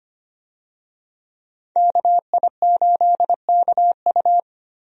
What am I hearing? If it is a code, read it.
Morse code, heard as a keyed tone, about 25 wpm, 690 Hz. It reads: KI8KU
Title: USA Call Signs - 5 Characters - 25wpm